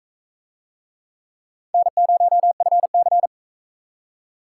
Morse code, heard as a keyed tone, about 41 wpm, 700 Hz. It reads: N0FC